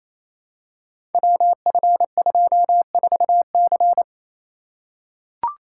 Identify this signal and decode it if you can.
Morse code, heard as a keyed tone, about 28 wpm, 700 Hz. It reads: WF24C
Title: Code Groups - 5 Letter:Numbers - 28wpm